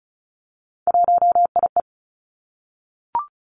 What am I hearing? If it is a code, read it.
Morse code, heard as a keyed tone, about 36 wpm, 700 Hz. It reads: 1IE
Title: Three Challenging Letter-Number Combinations 35wpm